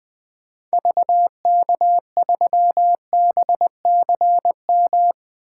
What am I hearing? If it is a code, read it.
Morse code, heard as a keyed tone, about 20 wpm, 700 Hz. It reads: VK3BCM
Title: 2023 SOTA Activator Call Signs - 20wpm